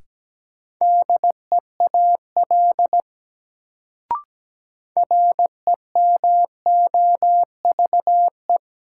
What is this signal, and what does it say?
Morse code, heard as a keyed tone, about 17 wpm, 700 Hz. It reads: DEAL REMOVE
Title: Sentences from Top 800 Words - New Word Drill - 17wpm